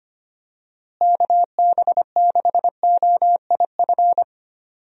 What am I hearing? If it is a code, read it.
Morse code, heard as a keyed tone, about 25 wpm, 700 Hz. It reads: KB6OIF
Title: USA Call Signs - 6 Characters - Repeated 2x - 25wpm